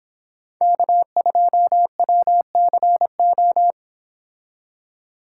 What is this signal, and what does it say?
Morse code, heard as a keyed tone, about 25 wpm, 700 Hz. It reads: K2WCO